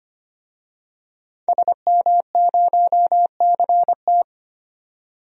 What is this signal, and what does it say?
Morse code, heard as a keyed tone, about 25 wpm, 700 Hz. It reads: SM0CT
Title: DXCC-100 Call Signs - 5 Characters - 25wpm